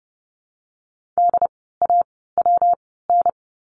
Morse code, heard as a keyed tone, about 30 wpm, spaced as for 17 wpm, 700 Hz. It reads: DAWN
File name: course / 30:10wpm / 115 Lesson 19f - Letter w - Words 30wpm